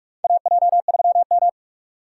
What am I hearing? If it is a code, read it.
Morse code, heard as a keyed tone, about 42 wpm, 700 Hz. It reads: AJ3M
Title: USA Call Signs - 4x Character Spacing - 45wpm